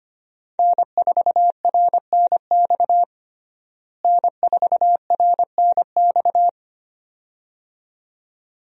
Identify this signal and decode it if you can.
Morse code, heard as a keyed tone, about 25 wpm, 700 Hz. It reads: N4RNX N4RNX
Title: USA Call Signs - 5 Characters - Repeated 2x - 25wpm